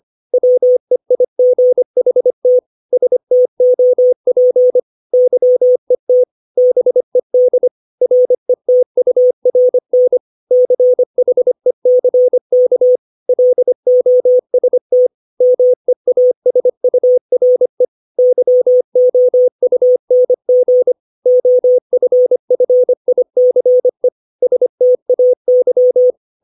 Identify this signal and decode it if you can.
Morse code, heard as a keyed tone, about 25 wpm, 500 Hz. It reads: WEIGHT STOP YET BED RETURN CHECK LOST MEASURE YOUNG OFFICE STAY
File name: 632 words / 632 Words 25WPM